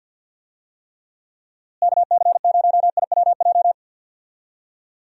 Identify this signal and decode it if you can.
Morse code, heard as a keyed tone, about 51 wpm, 700 Hz. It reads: KC0IWJ
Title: USA Call Signs - 6 Characters - Repeated 3x - 50wpm